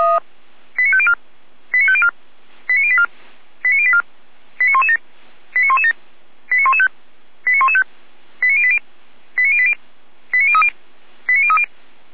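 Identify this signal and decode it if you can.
Five-tone selective call (ZVEI), heard as quick runs of five tones, tones of about 70 ms. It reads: 89504 89504 89084 89084 89117 89117 89115 89115 89080 89080 89030 89030
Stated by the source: Five-tone ZVEI selective call received by SDR.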